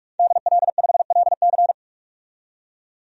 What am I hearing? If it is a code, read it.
Morse code, heard as a keyed tone, about 46 wpm, 700 Hz. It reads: DL5LC
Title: DXCC-100 Call Signs - 5 Characters - 45wpm